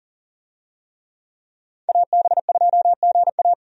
Morse code, heard as a keyed tone, about 40 wpm, 700 Hz. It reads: AB2GA
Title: USA Call Signs - 5 Characters - 40wpm